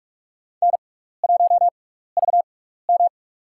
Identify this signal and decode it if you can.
Morse code, heard as a keyed tone, about 45 wpm, spaced as for 15 wpm, 700 Hz. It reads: N1VM
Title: USA Call Signs - 4 Characters - 6x Character Spacing - 45wpm